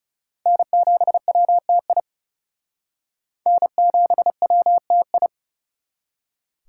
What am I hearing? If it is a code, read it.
Morse code, heard as a keyed tone, about 39 wpm, 700 Hz. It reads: N7WTI N7WTI